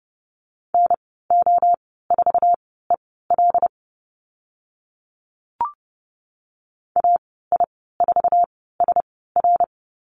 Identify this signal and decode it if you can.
Morse code, heard as a keyed tone, about 30 wpm, spaced as for 17 wpm, 700 Hz. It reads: NO4EL AI4SR
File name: Lesson 14g - Number 2 - Call Signs 30wpm